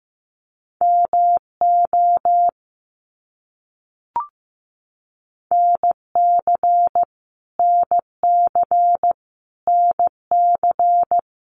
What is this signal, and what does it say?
Morse code, heard as a keyed tone, about 15 wpm, 700 Hz. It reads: MO NC NC NC
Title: US States Repeated 3x 15wpm